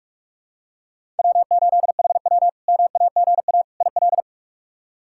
Morse code, heard as a keyed tone, about 47 wpm, 700 Hz. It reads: W8HW MAGA IL
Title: K1USN SST QSO Practice - 45wpm